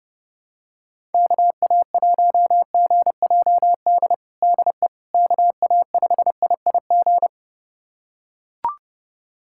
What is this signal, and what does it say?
Morse code, heard as a keyed tone, about 30 wpm, 700 Hz. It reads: KA1GJD DE KA5IIG
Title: USA Callsign DE Callsign - 6 Characters - 30wpm